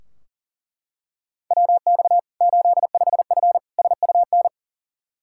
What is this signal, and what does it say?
Morse code, heard as a keyed tone, about 40 wpm, 700 Hz. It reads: WX 85F SUN
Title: Practice QSO - Exchange 2 - Info - 40wpm